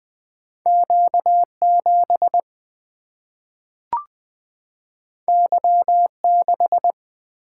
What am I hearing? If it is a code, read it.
Morse code, heard as a keyed tone, about 20 wpm, 700 Hz. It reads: Q7 Y6
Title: Single-Letter Single-Number 20wpm